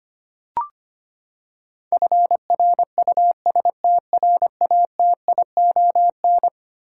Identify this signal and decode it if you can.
Morse code, heard as a keyed tone, about 25 wpm, 700 Hz. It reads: FRUSTRATION
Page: morse-code-ninja-practice-25wpm